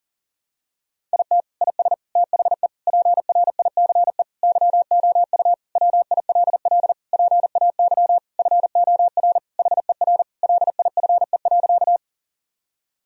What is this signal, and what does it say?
Morse code, heard as a keyed tone, about 37 wpm, 700 Hz. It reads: IT IS THE PRICE YOU WILL PAY FOR HER LIFE.
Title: Sentences from Top 1000 Words - Review Part 4 - 40wpm